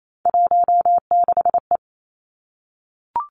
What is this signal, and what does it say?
Morse code, heard as a keyed tone, about 28 wpm, 700 Hz. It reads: 16E
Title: Three Challenging Letter-Number Combinations 28wpm